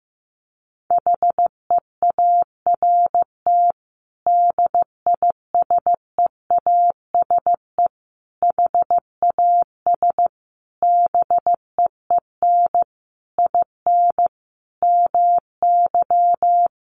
Morse code, heard as a keyed tone, about 15 wpm, 700 Hz. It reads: HEART DISEASE HAS BEEN IN MY
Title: Sentences from Top 1200 Words 15wpm